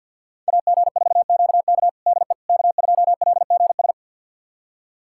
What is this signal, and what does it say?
Morse code, heard as a keyed tone, about 50 wpm, 700 Hz. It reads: AG4QK DE K2LGS
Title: USA Callsign DE Callsign - 5 Characters - 50wpm